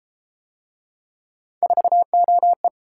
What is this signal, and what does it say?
Morse code, heard as a keyed tone, about 30 wpm, 700 Hz. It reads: 4OE